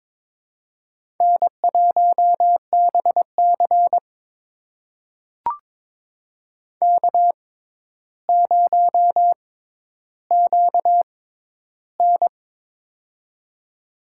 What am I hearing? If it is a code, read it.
Morse code, heard as a keyed tone, about 22 wpm, spaced as for 11 wpm, 700 Hz. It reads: N1BC K0QN
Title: USA Call Signs - 4 Characters - 6x Character Spacing - 22wpm